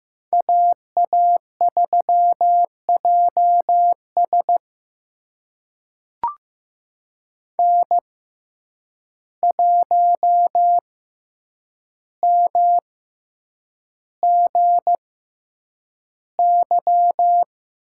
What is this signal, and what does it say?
Morse code, heard as a keyed tone, about 15 wpm, spaced as for 8 wpm, 700 Hz. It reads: AA3JS N1MGY
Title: USA Call Signs - 6x Character Spacing - 15wpm